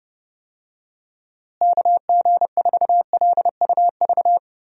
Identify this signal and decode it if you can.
Morse code, heard as a keyed tone, about 32 wpm, 700 Hz. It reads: KG4LUV
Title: USA Call Signs - 6 Characters - Repeated 3x - 30wpm